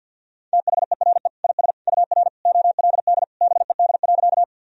Morse code, heard as a keyed tone, about 52 wpm, 700 Hz. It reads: THERE IS UR OLD BED.
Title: Sentences from Top 500 Words and Abbreviations 50wpm